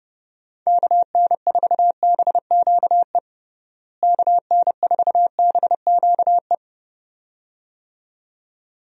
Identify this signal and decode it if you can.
Morse code, heard as a keyed tone, about 30 wpm, 700 Hz. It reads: KN4BQE KN4BQE
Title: USA Call Signs - 6 Characters - Repeated 2x - 30wpm